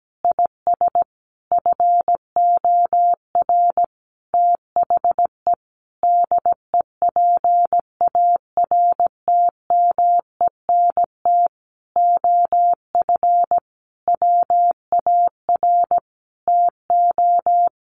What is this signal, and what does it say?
Morse code, heard as a keyed tone, about 17 wpm, 700 Hz. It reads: IS FOR THE DEPARTMENT OF WAR TO
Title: Sentences from Top 1500 Words 17wpm